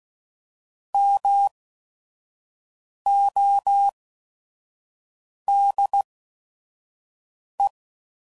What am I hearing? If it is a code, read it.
Morse code, heard as a keyed tone, about 16 wpm, spaced as for 5 wpm, 800 Hz. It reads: MODE